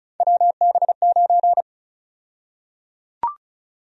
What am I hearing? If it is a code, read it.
Morse code, heard as a keyed tone, about 34 wpm, 700 Hz. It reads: WB9
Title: USA Call Sign Prefixes - Repeated 2x - 35wpm